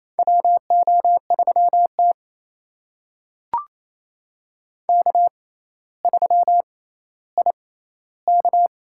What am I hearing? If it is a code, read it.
Morse code, heard as a keyed tone, about 28 wpm, spaced as for 14 wpm, 700 Hz. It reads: WO3T K3IK
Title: USA Call Signs - 4 Characters - 6x Character Spacing - 28wpm